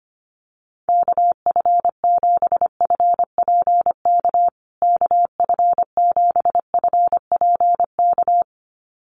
Morse code, heard as a keyed tone, about 25 wpm, 700 Hz. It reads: KF7FPK KF7FPK
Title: USA Call Signs Repeated 2x 25wpm